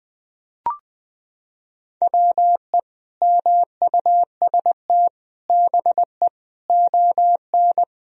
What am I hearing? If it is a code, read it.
Morse code, heard as a keyed tone, about 19 wpm, 700 Hz. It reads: WE MUST BE ON
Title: Sentences from Top 1000 Words - Review Part 4 - 20wpm